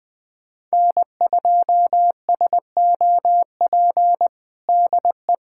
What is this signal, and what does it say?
Morse code, heard as a keyed tone, about 20 wpm, 700 Hz. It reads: N2SOP DE
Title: USA Callsign DE Callsign - 5 Characters - 20wpm